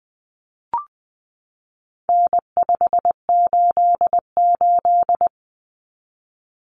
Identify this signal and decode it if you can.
Morse code, heard as a keyed tone, about 20 wpm, 700 Hz. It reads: N588
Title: Common Cut Numbers - 4-Digit 20wpm